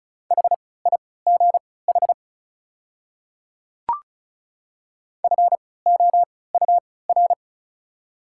Morse code, heard as a keyed tone, about 34 wpm, spaced as for 20 wpm, 700 Hz. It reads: HIGH FOUR
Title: 3x Character Spacing - Top 500 Words - 35wpm